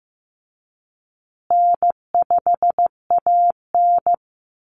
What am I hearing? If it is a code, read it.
Morse code, heard as a keyed tone, about 15 wpm, 700 Hz. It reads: N5AN
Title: Call Signs Repeated 3x - 15wpm